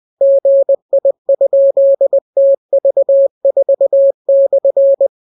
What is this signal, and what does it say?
Morse code, heard as a keyed tone, about 20 wpm, 550 Hz. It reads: GI?TV4/